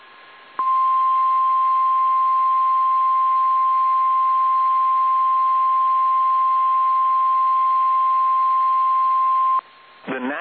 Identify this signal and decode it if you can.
Emergency Alert System (SAME) tones (the 1050 Hz attention tone). It is heard from 0.5 to 9.7 s.
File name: eas / KSDZKDJL / KSDZKDJL